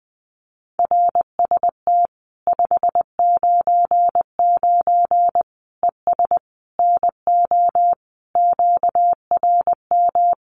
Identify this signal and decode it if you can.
Morse code, heard as a keyed tone, about 20 wpm, 700 Hz. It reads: RST 599 ES NO QRM
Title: Common QSO Phrases 20wpm